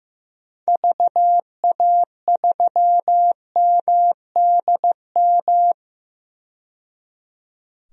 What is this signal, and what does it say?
Morse code, heard as a keyed tone, about 15 wpm, 700 Hz. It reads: VA3MDM
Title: DXCC-100 Call Signs - 6 Characters - 15wpm